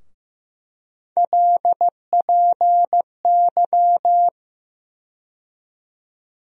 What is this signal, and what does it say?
Morse code, heard as a keyed tone, about 15 wpm, 700 Hz. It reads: LPY
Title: USA Call Sign Postfixes - 3 Letters 15wpm